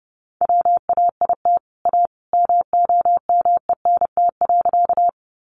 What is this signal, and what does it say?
Morse code, heard as a keyed tone, about 30 wpm, 700 Hz. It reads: WAIT A MOMENT.
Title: Sentences from Top 1000 Words 30wpm